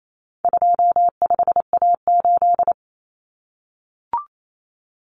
Morse code, heard as a keyed tone, about 28 wpm, 700 Hz. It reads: 25A8
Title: Common Cut Numbers - 4-Digit 28wpm